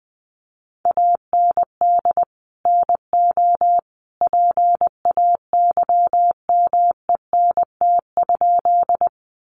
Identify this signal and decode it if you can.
Morse code, heard as a keyed tone, about 20 wpm, 700 Hz. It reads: AND NO PAYMENT?
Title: Sentences from Top 1200 Words 20wpm